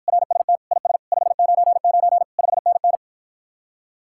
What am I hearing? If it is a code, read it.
Morse code, heard as a keyed tone, about 53 wpm, 700 Hz. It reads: RST IS 599 5NN